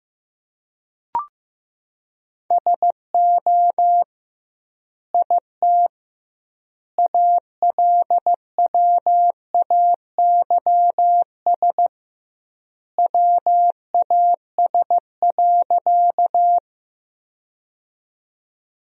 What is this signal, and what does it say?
Morse code, heard as a keyed tone, about 16 wpm, 700 Hz. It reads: SO IT ALWAYS WAS.
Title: 2x Word-Spacing - Sentences from Top 300 Words - 15wpm